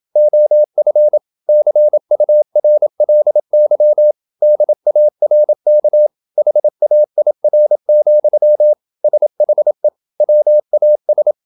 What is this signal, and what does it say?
Morse code, heard as a keyed tone, about 27 wpm, 600 Hz. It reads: OF CURLY DARK HAIR, SHE WAS
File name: Never 0027wpm